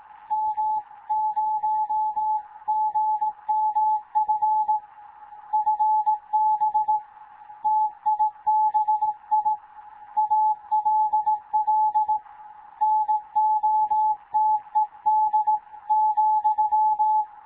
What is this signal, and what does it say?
Morse code, heard as a keyed tone, about 18 wpm, 860 Hz. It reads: M0GMF FB TIBI ALL NOTED ,
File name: Morze-HA2MN / hangp / 2 11 GYAKORLAT